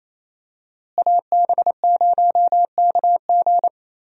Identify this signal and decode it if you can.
Morse code, heard as a keyed tone, about 27 wpm, 700 Hz. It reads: AB0KG
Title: USA Call Signs - 5 Characters - Repeated 3x - 28wpm